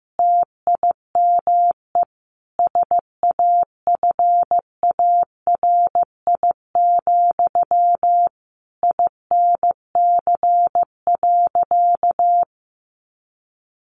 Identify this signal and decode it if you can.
Morse code, heard as a keyed tone, about 15 wpm, 700 Hz. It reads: TIME SAFARI, INC.
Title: A Sound Of Thunder 15wpm